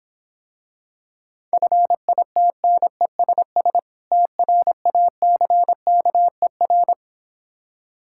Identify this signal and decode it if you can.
Morse code, heard as a keyed tone, about 26 wpm, 700 Hz. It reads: FITNESS TRACKER